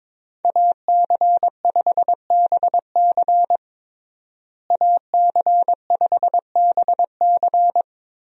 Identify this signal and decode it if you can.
Morse code, heard as a keyed tone, about 22 wpm, 700 Hz. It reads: AC5BC AC5BC
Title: USA Call Signs - 5 Characters - Repeated 2x - 22wpm